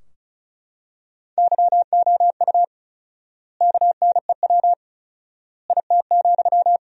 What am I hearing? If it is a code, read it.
Morse code, heard as a keyed tone, about 34 wpm, 700 Hz. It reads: YOU KNEW IT,
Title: 4x Word-Spacing - Sentences from Top 500 Words - 35wpm